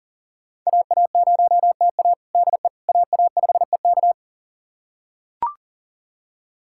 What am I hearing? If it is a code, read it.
Morse code, heard as a keyed tone, about 37 wpm, 700 Hz. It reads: AA0TA DE AA5EK
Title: USA Callsign DE Callsign - 5 Characters - 40wpm